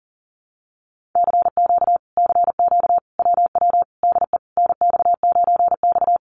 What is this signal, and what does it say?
Morse code, heard as a keyed tone, about 37 wpm, 700 Hz. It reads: CQ CQ WW DE NX9X
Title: Common QSO Phrases Encore 40wpm